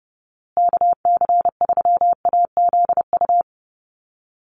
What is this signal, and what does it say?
Morse code, heard as a keyed tone, about 30 wpm, 700 Hz. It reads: KC3AZU
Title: USA Call Signs Repeated 2x 30wpm